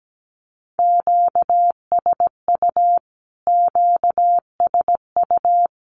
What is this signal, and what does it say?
Morse code, heard as a keyed tone, about 17 wpm, 700 Hz. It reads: QSU QSU
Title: Q-Codes Repeated 3x 17wpm